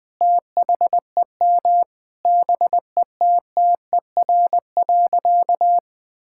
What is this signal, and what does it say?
Morse code, heard as a keyed tone, about 21 wpm, 700 Hz. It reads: THEM BETTER.
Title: Sentences from Top 1600 Words 20wpm